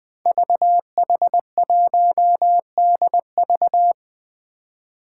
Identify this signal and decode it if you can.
Morse code, heard as a keyed tone, about 20 wpm, 700 Hz. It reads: VH1DV
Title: Five Challenging Letter-Number Combinations 20wpm